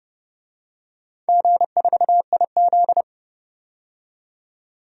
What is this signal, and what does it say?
Morse code, heard as a keyed tone, about 30 wpm, 700 Hz. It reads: G4IZ
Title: DXCC-100 Call Signs - 4 Characters - 30wpm